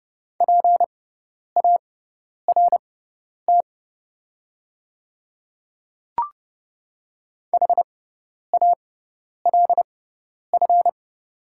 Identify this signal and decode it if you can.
Morse code, heard as a keyed tone, about 30 wpm, spaced as for 10 wpm, 700 Hz. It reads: PART HALF
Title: 6x Character Spacing - Top 500 Words - 30wpm